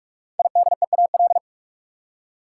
Morse code, heard as a keyed tone, about 46 wpm, 700 Hz. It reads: IDEAL
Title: Five Letter Words - 45wpm